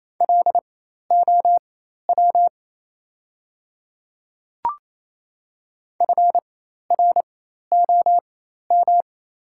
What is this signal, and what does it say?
Morse code, heard as a keyed tone, about 28 wpm, spaced as for 13 wpm, 700 Hz. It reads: LOW FROM
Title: 4x Character Spacing - Top 300 Words - 28wpm